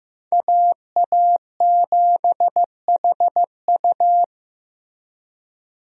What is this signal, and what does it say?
Morse code, heard as a keyed tone, about 15 wpm, 700 Hz. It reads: AA7HU
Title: USA Call Signs - 5 Characters - Repeated 2x - 15wpm